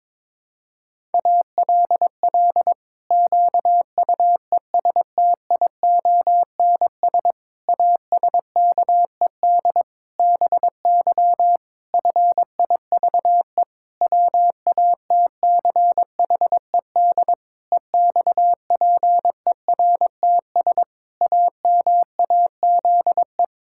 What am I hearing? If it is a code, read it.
Morse code, heard as a keyed tone, about 22 wpm, 700 Hz. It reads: ALL QUESTIONS ASKED BY FIVE WATCHED EXPERTS AMAZE
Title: Pangram Challenge v2 - 22wpm